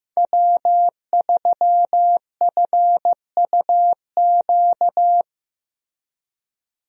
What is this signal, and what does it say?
Morse code, heard as a keyed tone, about 15 wpm, 700 Hz. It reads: W3FUQ
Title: USA Call Signs - 5 Characters - Repeated 2x - 15wpm